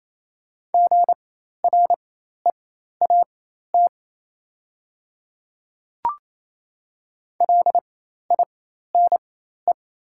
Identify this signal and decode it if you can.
Morse code, heard as a keyed tone, about 28 wpm, spaced as for 13 wpm, 700 Hz. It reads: GREAT LINE
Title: 4x Character Spacing - Top 200 Words - 28wpm